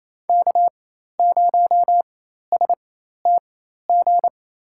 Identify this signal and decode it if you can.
Morse code, heard as a keyed tone, about 28 wpm, spaced as for 13 wpm, 700 Hz. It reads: K0STG
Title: USA Call Signs - 5 Characters - 4x Character Spacing - 28wpm